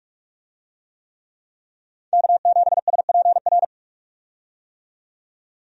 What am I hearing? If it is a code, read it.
Morse code, heard as a keyed tone, about 41 wpm, 700 Hz. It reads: K7SPR
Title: USA Call Signs - 5 Characters - 45wpm